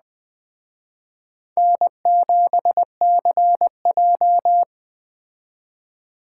Morse code, heard as a keyed tone, about 20 wpm, 700 Hz. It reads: N7CJ
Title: Speed Racing - USA 1x2 Call Signs 17wpm